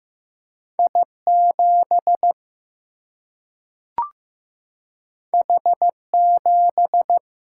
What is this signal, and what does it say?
Morse code, heard as a keyed tone, about 15 wpm, 700 Hz. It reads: I 7 H 7
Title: Single-Letter Single-Number 15wpm